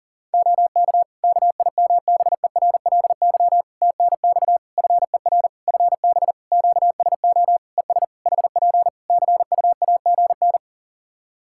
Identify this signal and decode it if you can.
Morse code, heard as a keyed tone, about 40 wpm, 700 Hz. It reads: OK KIMBERLY TNX FER FB QSO ES HP CUAGN
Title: Practice QSO - Exchange 4 - Ending - 40wpm